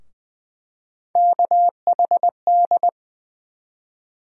Morse code, heard as a keyed tone, about 20 wpm, 700 Hz. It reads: KHD
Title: USA Call Sign Postfixes - 3 Letters 20wpm